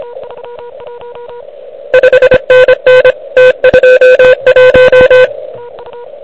Morse code, heard as a keyed tone, about 22 wpm, 500 Hz. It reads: 5NN T21
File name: ARCHIV / QCX / QCX Pileup